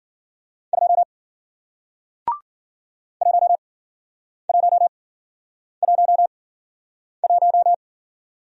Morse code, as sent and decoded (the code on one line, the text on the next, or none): ..--- / .---- .---- .---- .----
2 1111